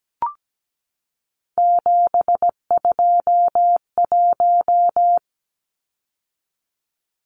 Morse code, heard as a keyed tone, about 17 wpm, 700 Hz. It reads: 721
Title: Numbers - 3-Digit 17wpm